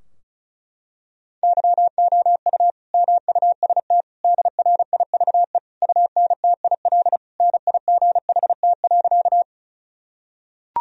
Morse code, heard as a keyed tone, about 36 wpm, 700 Hz. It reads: YOU MUST DRIVE UNTIL NIGHT.
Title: Sentences from Top 1000 Words - Review Part 2 - 35wpm